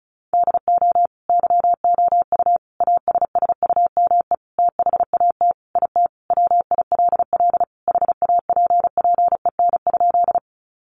Morse code, heard as a keyed tone, about 34 wpm, 700 Hz. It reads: DO YOU ASSUME THAT IT WILL HAPPEN?
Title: Sentences from Top 1500 Words 35wpm